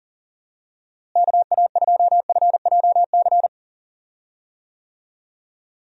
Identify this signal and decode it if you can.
Morse code, heard as a keyed tone, about 38 wpm, 700 Hz. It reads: KA2FJC
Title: USA Call Signs - 6 Characters - Repeated 2x - 40wpm